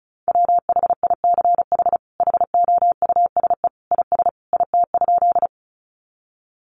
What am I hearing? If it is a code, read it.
Morse code, heard as a keyed tone, about 34 wpm, 700 Hz. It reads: WHICH HOUSE IS IT?
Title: Sentences from Top 200 Words 35wpm